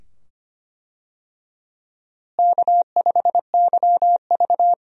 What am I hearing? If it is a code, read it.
Morse code, heard as a keyed tone, about 25 wpm, 700 Hz. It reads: K5YV
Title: USA Call Signs - 4 Characters - Repeated 2x - 25wpm